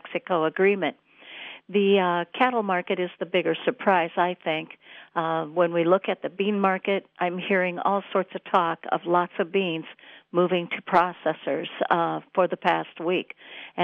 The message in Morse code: none

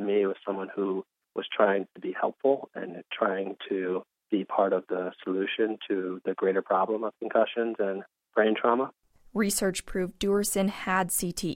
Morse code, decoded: none

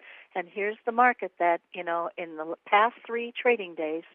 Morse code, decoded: none

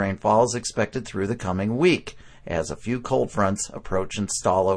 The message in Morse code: none

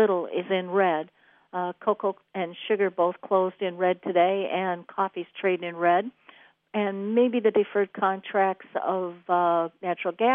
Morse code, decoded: none

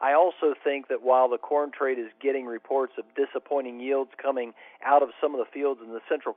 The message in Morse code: none

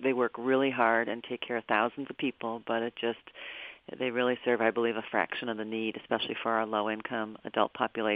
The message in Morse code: none

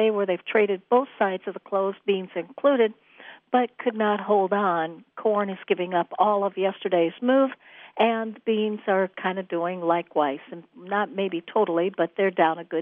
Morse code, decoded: none

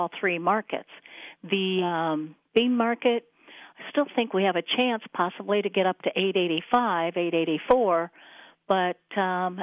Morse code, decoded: none

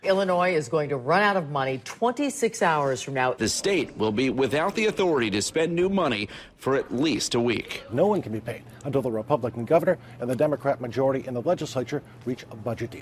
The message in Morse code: none

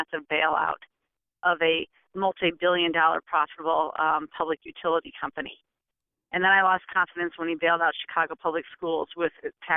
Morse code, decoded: none

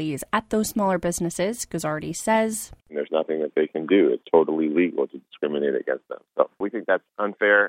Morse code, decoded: none

